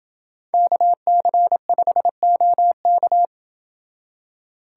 Morse code, heard as a keyed tone, about 29 wpm, 700 Hz. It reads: KC5OK